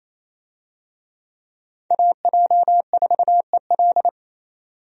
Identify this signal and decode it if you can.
Morse code, heard as a keyed tone, about 28 wpm, 700 Hz. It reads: AJ4EL
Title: USA Call Signs - 5 Characters - 28wpm